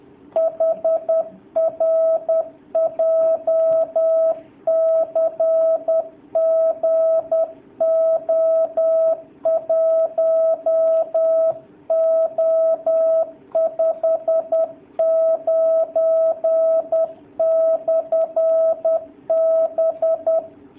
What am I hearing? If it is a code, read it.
Morse code, heard as a keyed tone, about 10 wpm, 660 Hz. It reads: HR JCG O1O59/B
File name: jh8cbh / 2019cw / sp10- w